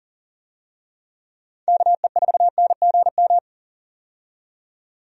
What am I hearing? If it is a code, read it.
Morse code, heard as a keyed tone, about 40 wpm, 700 Hz. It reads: KE4NGM
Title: USA Call Signs - 6 Characters - 40wpm